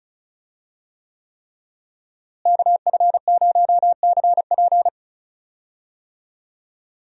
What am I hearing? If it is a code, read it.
Morse code, heard as a keyed tone, about 34 wpm, 700 Hz. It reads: KF0CP
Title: USA Call Signs - 5 Characters - Repeated 3x - 35wpm